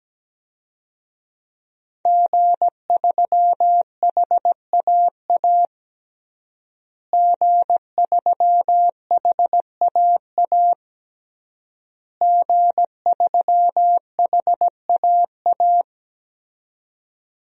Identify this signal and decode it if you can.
Morse code, heard as a keyed tone, about 17 wpm, 700 Hz. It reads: G3HAA G3HAA G3HAA
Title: DXCC-100 Call Signs - 5 Characters - Repeated 3x - 17wpm